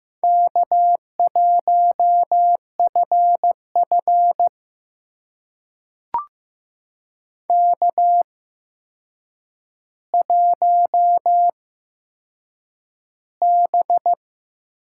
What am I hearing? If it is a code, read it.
Morse code, heard as a keyed tone, about 15 wpm, spaced as for 7 wpm, 700 Hz. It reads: K1FF K1B
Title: USA Call Signs - 4 Characters - 8x Character Spacing - 15wpm